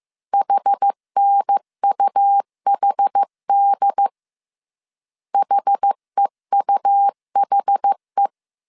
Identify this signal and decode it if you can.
Morse code, heard as a keyed tone, about 15 wpm, 800 Hz. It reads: HNUHD HEUHE
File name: CW / DET / 1 3 1500C S A 14wpm 2447